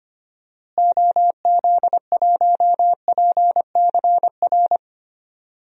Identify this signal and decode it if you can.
Morse code, heard as a keyed tone, about 25 wpm, 700 Hz. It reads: OZ1PCR
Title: DXCC-100 Call Signs - 6 Characters - Repeated 2x - 25wpm